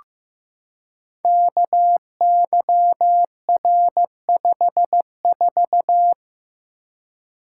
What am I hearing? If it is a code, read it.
Morse code, heard as a keyed tone, about 15 wpm, 700 Hz. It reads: KYR54
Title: Code Groups - 5 Letter:Numbers - 15wpm